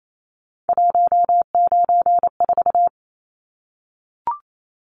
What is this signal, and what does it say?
Morse code, heard as a keyed tone, about 28 wpm, 700 Hz. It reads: 194
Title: Numbers - 3-Digit 28wpm